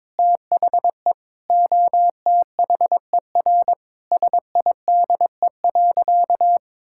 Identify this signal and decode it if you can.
Morse code, heard as a keyed tone, about 22 wpm, 700 Hz. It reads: THE OTHER SIDE.
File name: practice / 22wpm / Sentences with 4-Word Phrases - 22wpm